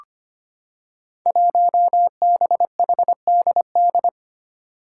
Morse code, heard as a keyed tone, about 25 wpm, 700 Hz. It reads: 1BHDD
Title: Five Challenging Letter-Number Combinations 25wpm